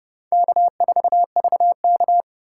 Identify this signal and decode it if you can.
Morse code, heard as a keyed tone, about 30 wpm, 700 Hz. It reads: K4VK
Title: USA Call Signs - 4 Characters - 3x Character Spacing - 30wpm